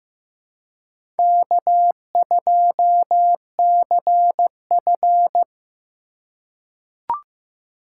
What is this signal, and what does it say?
Morse code, heard as a keyed tone, about 15 wpm, 700 Hz. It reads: K2CF
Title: USA Call Signs - 4 Characters - 4x Character Spacing - 15wpm